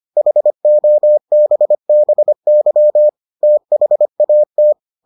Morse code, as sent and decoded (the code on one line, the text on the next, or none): .... --- -... -... -.-- / - .... .- -
HOBBY THAT